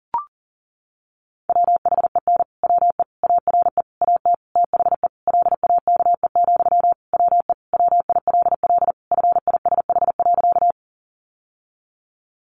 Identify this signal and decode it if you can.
Morse code, heard as a keyed tone, about 40 wpm, 700 Hz. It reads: WHEN WE ARE AT THE LAKE, WE WILL FISH.
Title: Sentences from Top 1100 Words 40wpm